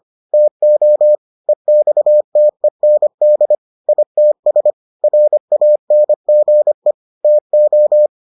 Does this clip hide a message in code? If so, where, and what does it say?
Yes: Morse code from 0.3 to 8.1 s, heard as a keyed tone, about 25 wpm, 600 Hz. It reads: TO EXTEND ITS RANGE TO